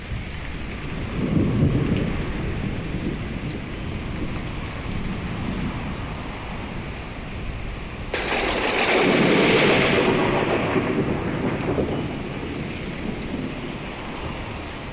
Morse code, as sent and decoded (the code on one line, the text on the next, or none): none